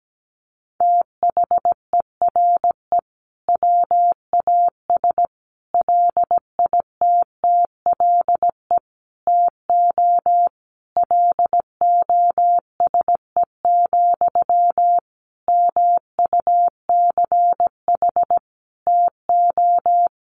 Morse code, heard as a keyed tone, about 17 wpm, 700 Hz. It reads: THERE WAS LITTLE TO LOSE, MUCH TO
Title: Sentences from Top 1100 Words 17wpm